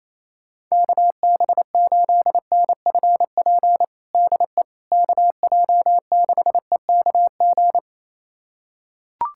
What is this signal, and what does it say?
Morse code, heard as a keyed tone, about 28 wpm, 700 Hz. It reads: KB8NFP DE KJ6EKG
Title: USA Callsign DE Callsign - 6 Characters - 28wpm